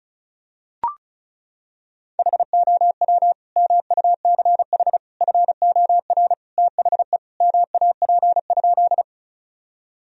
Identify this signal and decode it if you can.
Morse code, heard as a keyed tone, about 36 wpm, 700 Hz. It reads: HOW MUCH FOR THE MAP?
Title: Sentences from Top 1000 Words - Review Part 2 - 35wpm